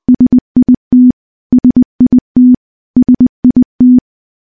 Morse code, heard as a keyed tone, about 20 wpm, 250 Hz. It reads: SIT SIT SIT
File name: morse-training-20WPM